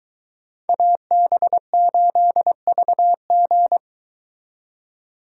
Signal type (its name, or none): Morse code